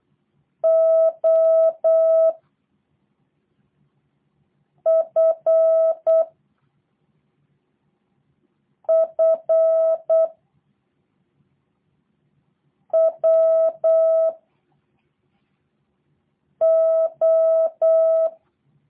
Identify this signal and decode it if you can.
Morse code, heard as a keyed tone, about 8 wpm, 650 Hz. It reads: O F F W O
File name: jh8cbh / 2023cw / review1-4